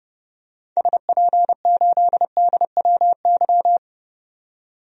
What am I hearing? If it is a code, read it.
Morse code, heard as a keyed tone, about 30 wpm, 700 Hz. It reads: SP8DWY
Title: DXCC-100 Call Signs - 6 Characters - 30wpm